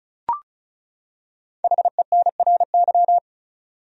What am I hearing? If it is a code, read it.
Morse code, heard as a keyed tone, about 34 wpm, 700 Hz. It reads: HENRY
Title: USA Common Names - 35wpm